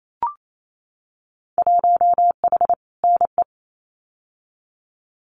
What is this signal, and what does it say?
Morse code, heard as a keyed tone, about 28 wpm, 700 Hz. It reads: 1H NE
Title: Winter Field Day Exchanges Encore 28wpm